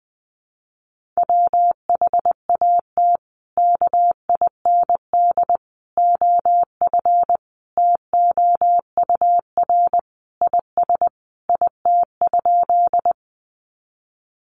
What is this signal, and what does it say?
Morse code, heard as a keyed tone, about 20 wpm, 700 Hz. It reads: WHAT KIND OF TOUR IS IT?
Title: Sentences from Top 1400 Words 20wpm